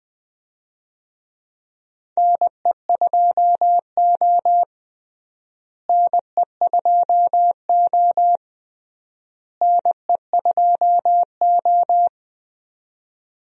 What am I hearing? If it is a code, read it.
Morse code, heard as a keyed tone, about 21 wpm, 700 Hz. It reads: NE2O NE2O NE2O
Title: USA Call Signs - 4 Characters - Repeated 3x - 20wpm